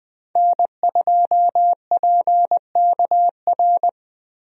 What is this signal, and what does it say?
Morse code, heard as a keyed tone, about 20 wpm, 700 Hz. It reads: N2PKR
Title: USA Call Signs - 5 Characters - Repeated 3x - 20wpm